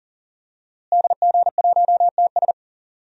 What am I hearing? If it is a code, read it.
Morse code, heard as a keyed tone, about 38 wpm, 700 Hz. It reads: DG1TS
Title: DXCC-100 Call Signs - 5 Characters - Repeated 2x - 40wpm